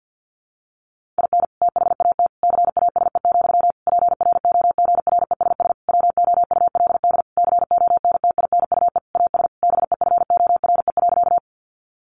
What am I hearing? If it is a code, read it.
Morse code, heard as a keyed tone, about 50 wpm, 700 Hz. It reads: IN THAT CASE, PROGRESS WOULD CONTINUE AS BEFORE.
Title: Sentences from Top 1400 Words 50wpm